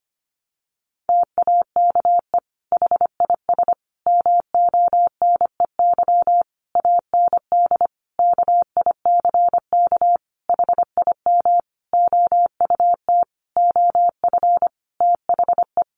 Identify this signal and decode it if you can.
Morse code, heard as a keyed tone, about 25 wpm, 700 Hz. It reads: TAKE HIS MONEY AND KICK HIM OUT OF THE
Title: Sentences from Top 1300 Words 25wpm